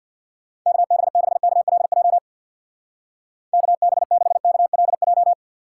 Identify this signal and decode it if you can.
Morse code, heard as a keyed tone, about 64 wpm, 700 Hz. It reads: KB6KLJ KB6KLJ